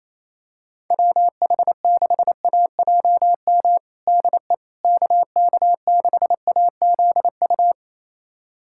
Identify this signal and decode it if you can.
Morse code, heard as a keyed tone, about 28 wpm, 700 Hz. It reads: WH6AJM DE KK6AZU
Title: USA Callsign DE Callsign 28wpm